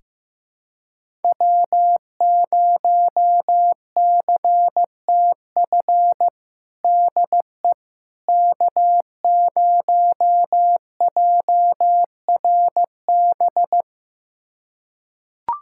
Morse code, heard as a keyed tone, about 16 wpm, 700 Hz. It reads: W0CTF DE K0JRB
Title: USA Callsign DE Callsign - 5 Characters - 15wpm